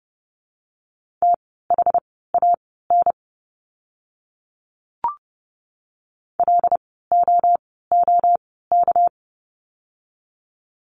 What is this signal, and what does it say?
Morse code, heard as a keyed tone, about 30 wpm, spaced as for 17 wpm, 700 Hz. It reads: THAN LOOK
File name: course / 30:10wpm / 237 Lesson 38f - Letter q - Words 30wpm